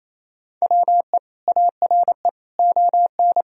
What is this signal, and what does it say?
Morse code, heard as a keyed tone, about 28 wpm, 700 Hz. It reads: WE ARE ON
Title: Sentences from Top 1000 Words - Review Part 4 - 28wpm